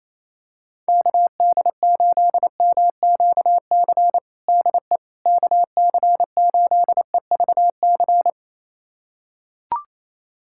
Morse code, heard as a keyed tone, about 28 wpm, 700 Hz. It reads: KD8MQC DE KC8EVC
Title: USA Callsign DE Callsign 28wpm